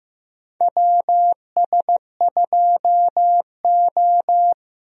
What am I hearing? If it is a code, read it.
Morse code, heard as a keyed tone, about 15 wpm, 700 Hz. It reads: WS2O
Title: USA Call Signs - 4 Characters - Repeated 3x - 15wpm